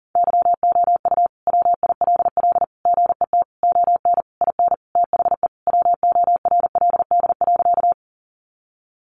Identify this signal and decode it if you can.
Morse code, heard as a keyed tone, about 40 wpm, 700 Hz. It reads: YOU WILL GET ON IN THE WORLD.
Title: Sentences from Top 200 Words 40wpm